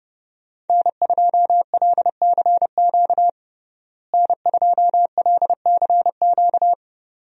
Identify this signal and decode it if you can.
Morse code, heard as a keyed tone, about 30 wpm, 700 Hz. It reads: N2LCQ N2LCQ
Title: USA Call Signs - 5 Characters - Repeated 2x - 30wpm